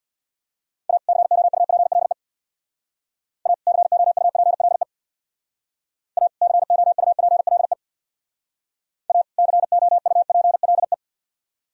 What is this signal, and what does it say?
Morse code, heard as a keyed tone, about 63 wpm, 700 Hz. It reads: A COUPLE A COUPLE A COUPLE A COUPLE